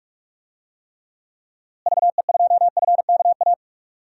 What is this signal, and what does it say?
Morse code, heard as a keyed tone, about 46 wpm, 700 Hz. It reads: VE2FKA
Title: DXCC-100 Call Signs - 6 Characters - Repeated 3x - 45wpm